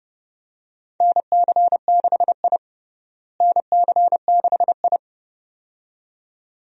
Morse code, heard as a keyed tone, about 30 wpm, 700 Hz. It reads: NC6I NC6I
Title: USA Call Signs - 4 Characters - Repeated 2x - 30wpm